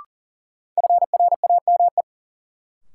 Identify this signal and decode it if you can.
Morse code, heard as a keyed tone, about 40 wpm, 700 Hz. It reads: FRAME
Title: Sentences from Top 1000 Words - Review Part 6 - New Word Drill - 40wpm